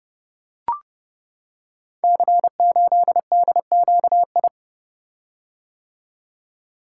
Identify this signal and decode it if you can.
Morse code, heard as a keyed tone, about 30 wpm, 700 Hz. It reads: C8DQI
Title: Code Groups - 5 Letter:Numbers - 30wpm